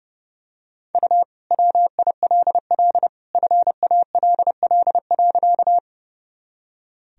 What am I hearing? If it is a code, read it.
Morse code, heard as a keyed tone, about 32 wpm, 700 Hz. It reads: U WILL FALL.
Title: Sentences from Top 500 Words and Abbreviations 30wpm